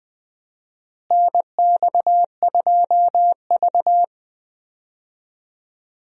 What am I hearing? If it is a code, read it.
Morse code, heard as a keyed tone, about 20 wpm, 700 Hz. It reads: NX2V
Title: USA Call Signs - 4 Characters - Repeated 3x - 20wpm